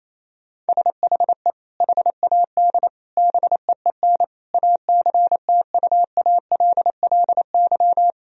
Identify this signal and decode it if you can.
Morse code, heard as a keyed tone, about 28 wpm, 700 Hz. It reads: SHE HAD BEEN ACTUALLY
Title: Sentences from Top 1000 Words - Review Part 4 - 28wpm